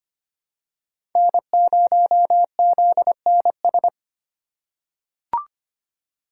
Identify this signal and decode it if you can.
Morse code, heard as a keyed tone, about 25 wpm, 700 Hz. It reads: N0ZNS
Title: USA Call Signs - 5 Characters - 4x Character Spacing - 25wpm